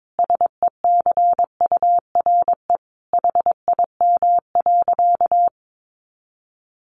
Morse code, heard as a keyed tone, about 22 wpm, 700 Hz. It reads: SECURE HIM.
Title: Sentences from Top 1200 Words 22wpm